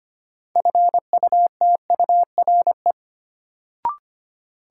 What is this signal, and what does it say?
Morse code, heard as a keyed tone, about 25 wpm, 700 Hz. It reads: FUTURE
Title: Sentences from Top 700 Words - New Word Drill - 25wpm